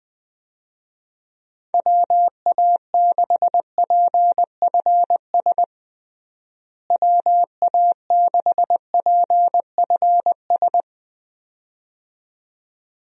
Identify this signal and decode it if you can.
Morse code, heard as a keyed tone, about 21 wpm, 700 Hz. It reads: WA6PFS WA6PFS
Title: USA Call Signs - 6 Characters - Repeated 2x - 20wpm